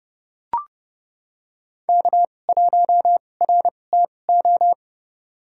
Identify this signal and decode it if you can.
Morse code, heard as a keyed tone, about 31 wpm, spaced as for 22 wpm, 700 Hz. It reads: K1RTO
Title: USA Call Signs - 5 Characters - 2x Character Spacing - 30wpm